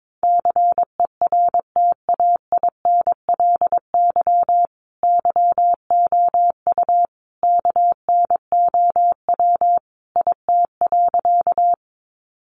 Morse code, heard as a keyed tone, about 22 wpm, 700 Hz. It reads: CERTAINLY YOU KNOW IT.
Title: Sentences from Top 900 Words 22wpm